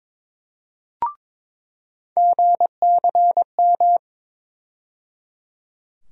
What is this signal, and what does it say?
Morse code, heard as a keyed tone, about 22 wpm, 700 Hz. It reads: GCM
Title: USA Call Sign Postfixes - 3 Letters 22wpm